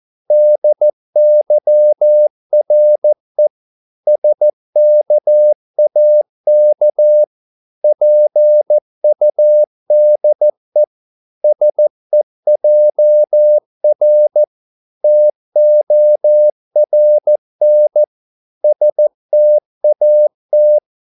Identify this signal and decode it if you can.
Morse code, heard as a keyed tone, about 14 wpm, 600 Hz. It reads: DYRE SKAK PUDE SEJR TORN STAT
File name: Korte ord DK 14wpm